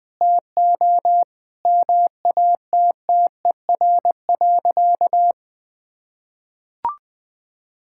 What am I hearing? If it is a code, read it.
Morse code, heard as a keyed tone, about 20 wpm, 700 Hz. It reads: TO MATTER.
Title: Sentences with 4-Word Phrases - 20wpm